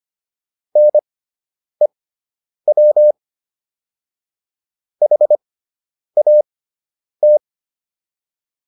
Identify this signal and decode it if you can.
Morse code, heard as a keyed tone, about 25 wpm, spaced as for 9 wpm, 600 Hz. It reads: NEW HAT